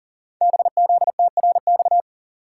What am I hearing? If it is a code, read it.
Morse code, heard as a keyed tone, about 42 wpm, 700 Hz. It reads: BZTRX
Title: Code Groups - 5 Letter:Numbers - 40wpm